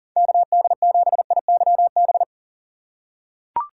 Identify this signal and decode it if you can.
Morse code, heard as a keyed tone, about 40 wpm, 700 Hz. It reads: KD7IYB
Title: USA Call Signs - 6 Characters - 4x Character Spacing - 40wpm